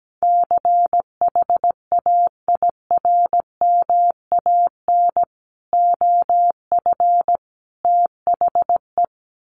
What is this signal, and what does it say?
Morse code, heard as a keyed tone, about 17 wpm, 700 Hz. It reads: CHAIRMAN OF THE
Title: Sets of 3 Words 17wpm